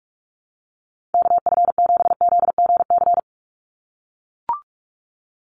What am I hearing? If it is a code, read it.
Morse code, heard as a keyed tone, about 44 wpm, 700 Hz. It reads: KF7ZGC
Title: USA Call Signs 45wpm